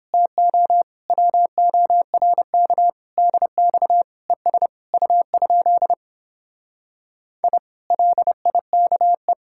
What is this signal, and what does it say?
Morse code, heard as a keyed tone, about 29 wpm, 700 Hz. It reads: TO WORK DX ES U? I LIKE